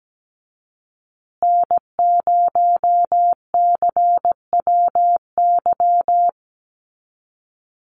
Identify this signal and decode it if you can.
Morse code, heard as a keyed tone, about 17 wpm, 700 Hz. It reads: N0CWY
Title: USA Call Signs Repeated 3x 17wpm